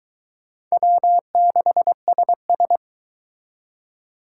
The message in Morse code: W6SS

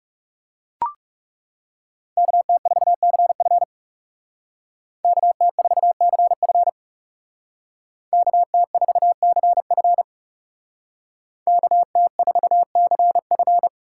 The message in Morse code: KT4CF KT4CF KT4CF KT4CF

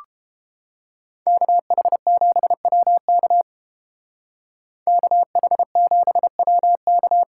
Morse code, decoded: KH7WK KH7WK

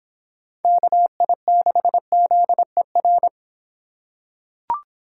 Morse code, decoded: KI6ZER

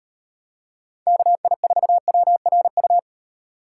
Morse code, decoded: KI4WRU

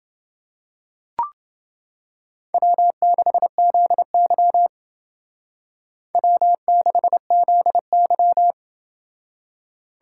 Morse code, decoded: W6ZY W6ZY

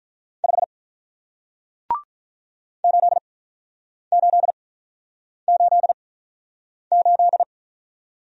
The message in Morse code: 5 8888

